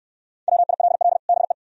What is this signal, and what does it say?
Morse code, heard as a keyed tone, about 69 wpm, 700 Hz. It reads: WILL BE